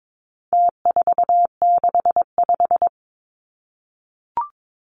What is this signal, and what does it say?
Morse code, heard as a keyed tone, about 22 wpm, 700 Hz. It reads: T465
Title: Numbers - Contest Style - 4-Digit 22wpm